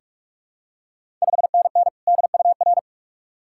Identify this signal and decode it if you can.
Morse code, heard as a keyed tone, about 44 wpm, 690 Hz. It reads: 5NN DUR
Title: Ontario QSO Party Exchange 45wpm